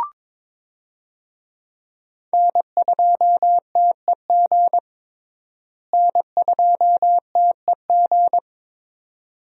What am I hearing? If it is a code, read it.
Morse code, heard as a keyed tone, about 22 wpm, 700 Hz. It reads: N2TEG N2TEG
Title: USA Call Signs - 5 Characters - Repeated 2x - 22wpm